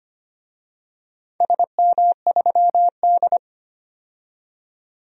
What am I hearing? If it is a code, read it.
Morse code, heard as a keyed tone, about 25 wpm, 700 Hz. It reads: SM3D
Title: DXCC-100 Call Signs - 4 Characters - 25wpm